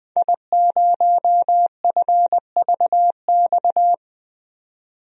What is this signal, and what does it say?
Morse code, heard as a keyed tone, about 20 wpm, 700 Hz. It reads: I0FVX
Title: DXCC-100 Call Signs - 5 Characters - 20wpm